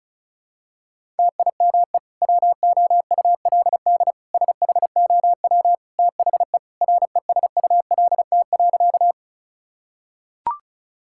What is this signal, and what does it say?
Morse code, heard as a keyed tone, about 36 wpm, 700 Hz. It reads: TIME WOULD SHOW THE RESULT.
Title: Sentences from Top 1000 Words - Review Part 5 - 35wpm